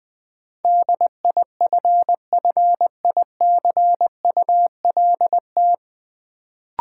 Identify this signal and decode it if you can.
Morse code, heard as a keyed tone, about 20 wpm, 700 Hz. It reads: DIFFICULT
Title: Sets of 4 Words - New Word Drill - 20wpm